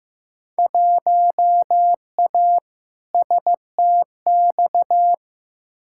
Morse code, heard as a keyed tone, about 15 wpm, 700 Hz. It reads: 1A STX
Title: Field Day Exchanges - 15wpm